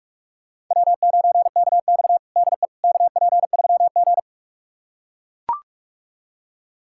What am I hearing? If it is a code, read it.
Morse code, heard as a keyed tone, about 42 wpm, 700 Hz. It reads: W9KX DE KP3G
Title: USA Callsign DE Callsign - 4 Characters - 45wpm